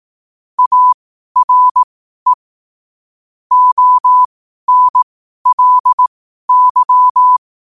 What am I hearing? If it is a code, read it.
Morse code, heard as a keyed tone, about 18 wpm, spaced as for 12 wpm, 1000 Hz. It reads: ARE ONLY
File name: qotdfiles / 12wpm